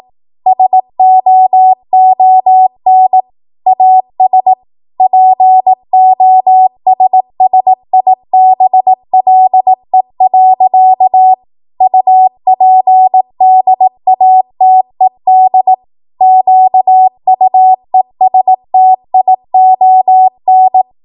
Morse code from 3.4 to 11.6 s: AS POSSIBLE.